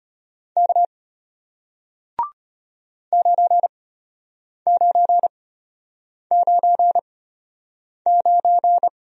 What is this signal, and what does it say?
Morse code, heard as a keyed tone, about 34 wpm, spaced as for 8 wpm, 700 Hz. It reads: K 9999